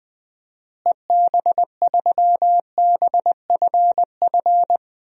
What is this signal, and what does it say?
Morse code, heard as a keyed tone, about 19 wpm, 700 Hz. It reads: EB3BFF
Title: DXCC-100 Call Signs - 6 Characters - Repeated 3x - 20wpm